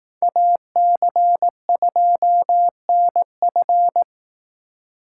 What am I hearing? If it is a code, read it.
Morse code, heard as a keyed tone, about 19 wpm, 700 Hz. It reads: AC2NF